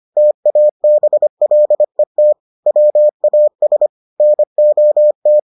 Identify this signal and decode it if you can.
Morse code, heard as a keyed tone, about 25 wpm, 600 Hz. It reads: TABLET WAS NOT